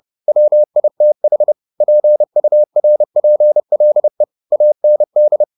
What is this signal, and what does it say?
Morse code, heard as a keyed tone, about 30 wpm, 600 Hz. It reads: WITH PURPLE AND